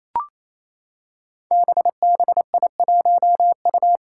Morse code, heard as a keyed tone, about 27 wpm, 700 Hz. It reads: BBI1U
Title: Five Challenging Letter-Number Combinations 28wpm